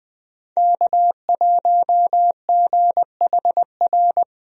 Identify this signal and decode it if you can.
Morse code, heard as a keyed tone, about 20 wpm, 700 Hz. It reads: K1GHR
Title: USA Call Signs - 5 Characters - 20wpm